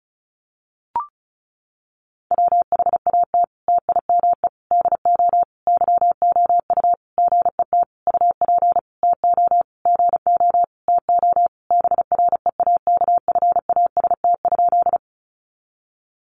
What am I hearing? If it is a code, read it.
Morse code, heard as a keyed tone, about 34 wpm, 700 Hz. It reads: WHAT TIME DO YOU GET UP TO GO TO BREAKFAST?
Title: Sentences from Top 1500 Words 35wpm